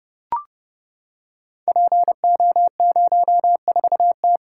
Morse code, heard as a keyed tone, about 29 wpm, 700 Hz. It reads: PO04T
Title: Code Groups - 5 Letter:Numbers - 30wpm